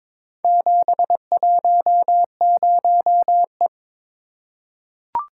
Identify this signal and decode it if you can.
Morse code, heard as a keyed tone, about 22 wpm, 700 Hz. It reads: 710E
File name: practice / 22wpm / Four Letter-Number Combinations - 22wpm